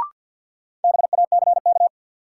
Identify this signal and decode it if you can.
Morse code, heard as a keyed tone, about 53 wpm, 700 Hz. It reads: BACK
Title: Sentences from Top 200 Words - New Word Drill - 50wpm